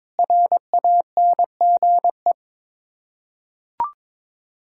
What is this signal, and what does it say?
Morse code, heard as a keyed tone, about 22 wpm, 700 Hz. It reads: RANGE
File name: Sets of 9 Words - 22wpm